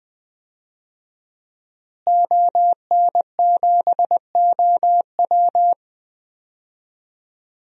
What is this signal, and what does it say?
Morse code, heard as a keyed tone, about 20 wpm, 700 Hz. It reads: ON7OW
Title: DXCC-100 Call Signs - 5 Characters - Repeated 3x - 20wpm